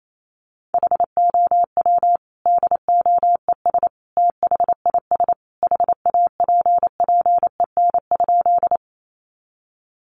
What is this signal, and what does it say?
Morse code, heard as a keyed tone, about 28 wpm, 700 Hz. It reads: HOW DOES THIS HAPPEN?
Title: Sentences from Top 400 Words 28wpm